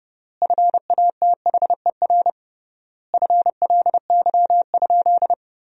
Morse code, heard as a keyed tone, about 30 wpm, 700 Hz. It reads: FATHER FLY?
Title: 3x Word-Spacing - Sentences from Top 500 Words - 30wpm